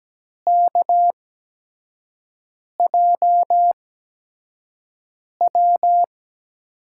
Morse code, heard as a keyed tone, about 17 wpm, spaced as for 5 wpm, 700 Hz. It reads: KJW
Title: USA Call Signs - 8x Character Spacing - 17wpm